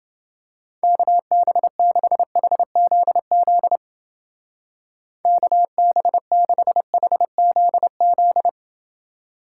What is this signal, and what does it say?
Morse code, heard as a keyed tone, about 28 wpm, 700 Hz. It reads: KB6HZZ KB6HZZ